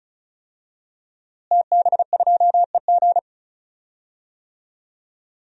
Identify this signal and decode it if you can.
Morse code, heard as a keyed tone, about 36 wpm, 700 Hz. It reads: TB2EG
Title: DXCC-100 Call Signs - 5 Characters - Repeated 3x - 35wpm